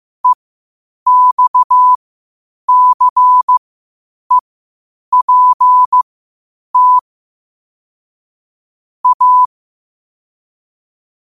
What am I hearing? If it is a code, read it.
Morse code, heard as a keyed tone, about 15 wpm, spaced as for 8 wpm, 1000 Hz. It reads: EXCEPT A
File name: 8wpm